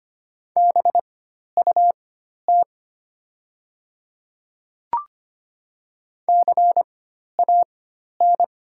Morse code, heard as a keyed tone, about 25 wpm, spaced as for 12 wpm, 700 Hz. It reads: BUT CAN
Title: 4x Character Spacing - Top 100 Words - 25wpm